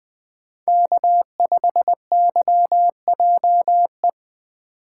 Morse code, heard as a keyed tone, about 20 wpm, 700 Hz. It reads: K5YJE